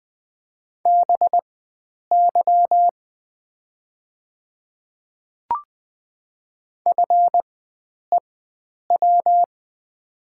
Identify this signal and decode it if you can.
Morse code, heard as a keyed tone, about 20 wpm, spaced as for 9 wpm, 690 Hz. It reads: BY FEW
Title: 4x Character Spacing - Top 300 Words - 20wpm